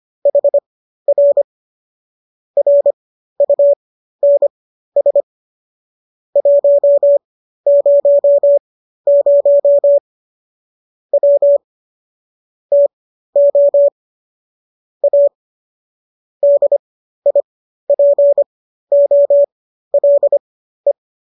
Morse code, heard as a keyed tone, about 25 wpm, spaced as for 13 wpm, 570 Hz. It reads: HR RUNS 100 W TO A DIPOLE